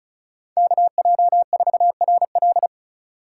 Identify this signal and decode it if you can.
Morse code, heard as a keyed tone, about 45 wpm, 700 Hz. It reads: KJ4RL